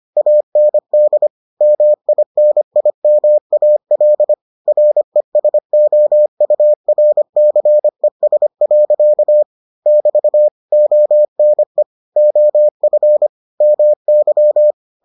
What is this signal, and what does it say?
Morse code, heard as a keyed tone, about 25 wpm, 600 Hz. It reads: AND MINIMAL RESOURCES. <BT> ONE OF MY